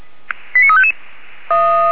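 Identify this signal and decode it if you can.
Five-tone selective call (ZVEI), heard as quick runs of five tones, tones of about 70 ms. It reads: 89355
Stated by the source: ZVEI five-tone selective call 89357 received by SDR, from the audio archive of Feuerwehr Orlamünde.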